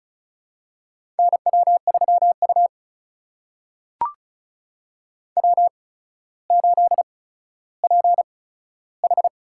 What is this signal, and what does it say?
Morse code, heard as a keyed tone, about 36 wpm, spaced as for 15 wpm, 700 Hz. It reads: NW3U W8PH
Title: USA Call Signs - 8x Character Spacing - 35wpm